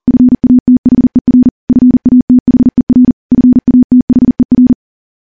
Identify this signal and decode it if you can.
Morse code, heard as a keyed tone, about 43 wpm, 250 Hz. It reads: FATHER FATHER FATHER
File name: morse / morse-training-40WPM